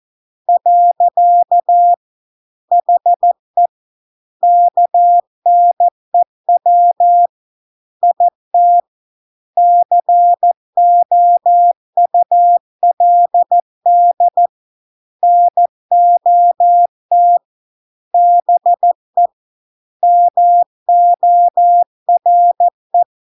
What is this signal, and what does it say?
Morse code, heard as a keyed tone, about 14 wpm, 700 Hz. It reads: . HE KNEW IT COULD NOT BE MORE